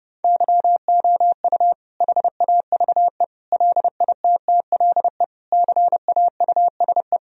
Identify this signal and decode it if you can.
Morse code, heard as a keyed tone, about 32 wpm, 700 Hz. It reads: YOU HAVE LITTLE CAUSE
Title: Sentences from Top 1000 Words - Review Part 1 - 30wpm